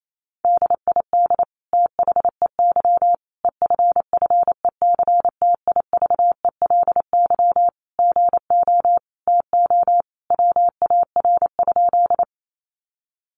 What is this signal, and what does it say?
Morse code, heard as a keyed tone, about 28 wpm, 700 Hz. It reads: DID THEY EFFECTIVELY GO TO WAR?
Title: Sentences from Top 1400 Words 28wpm